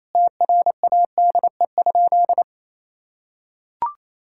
Morse code, heard as a keyed tone, about 28 wpm, 700 Hz. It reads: TRADE?
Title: Sentences from Top 1000 Words - Review Part 6 - 28wpm